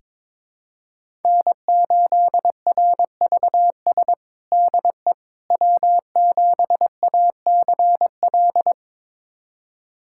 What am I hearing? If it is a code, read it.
Morse code, heard as a keyed tone, about 21 wpm, 700 Hz. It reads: N8RVS DE W7ACL
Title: USA Callsign DE Callsign - 5 Characters - 22wpm